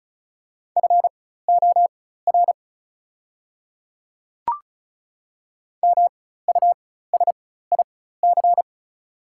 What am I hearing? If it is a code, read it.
Morse code, heard as a keyed tone, about 34 wpm, spaced as for 16 wpm, 700 Hz. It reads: FOR MUSIC
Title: 4x Character Spacing - Top 300 Words - 35wpm